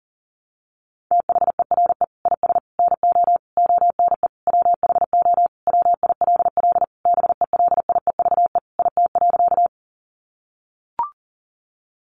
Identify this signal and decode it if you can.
Morse code, heard as a keyed tone, about 37 wpm, 700 Hz. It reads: THERE IS NO ONE WHO WILL BELIEVE IT.
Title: Sentences from Top 1000 Words 40wpm